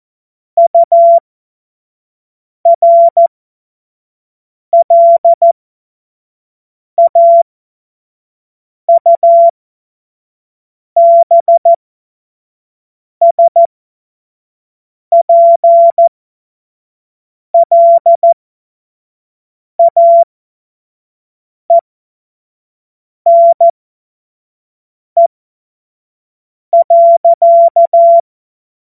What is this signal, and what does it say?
Morse code, heard as a keyed tone, about 14 wpm, spaced as for 5 wpm, 680 Hz. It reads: URLAUBSPLAENE.
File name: Morsekurs / Witze / Witz 21. März 2023 25 F70